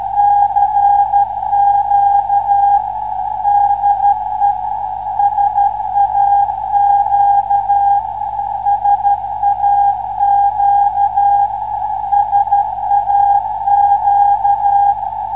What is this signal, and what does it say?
Morse code, heard as a keyed tone, about 12 wpm, 810 Hz. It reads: CQ DE SAQ SAQ SAQ